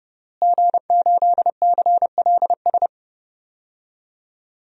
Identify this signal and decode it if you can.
Morse code, heard as a keyed tone, about 30 wpm, 700 Hz. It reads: G8CLS
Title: DXCC-100 Call Signs - 5 Characters - Repeated 3x - 30wpm